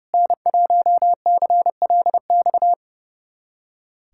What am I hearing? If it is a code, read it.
Morse code, heard as a keyed tone, about 30 wpm, 700 Hz. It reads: N1CLX
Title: USA Callsign DE Callsign - 5 Characters - 30wpm